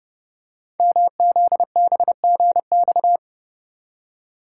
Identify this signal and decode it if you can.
Morse code, heard as a keyed tone, about 30 wpm, 700 Hz. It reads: MZBGX
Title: Code Groups - 5 Letters - 30wpm